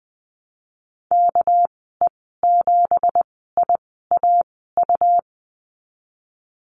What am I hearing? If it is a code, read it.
Morse code, heard as a keyed tone, about 20 wpm, spaced as for 15 wpm, 700 Hz. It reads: KE7IAU
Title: 207 Lesson 33g - Letter j - Call Signs 20wpm